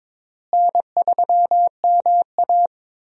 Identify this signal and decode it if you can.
Morse code, heard as a keyed tone, about 22 wpm, 700 Hz. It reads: N3MA
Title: USA Call Signs - 4 Characters - Repeated 3x - 22wpm